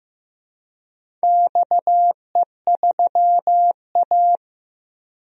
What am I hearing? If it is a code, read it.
Morse code, heard as a keyed tone, about 15 wpm, 700 Hz. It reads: XE3A
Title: DXCC-100 Call Signs - 4 Characters - Repeated 2x - 15wpm